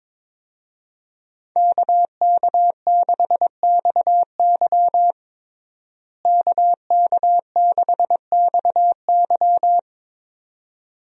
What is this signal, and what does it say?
Morse code, heard as a keyed tone, about 22 wpm, 700 Hz. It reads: KK6XY KK6XY
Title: USA Call Signs - 5 Characters - Repeated 2x - 22wpm